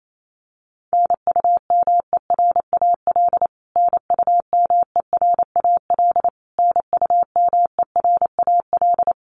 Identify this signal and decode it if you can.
Morse code, heard as a keyed tone, about 28 wpm, 700 Hz. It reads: NUMERAL NUMERAL NUMERAL
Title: Top 500 Words Repeated 28wpm